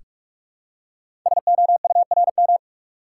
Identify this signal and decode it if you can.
Morse code, heard as a keyed tone, about 44 wpm, 690 Hz. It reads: SOURM
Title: Code Groups - 5 Letters - 45wpm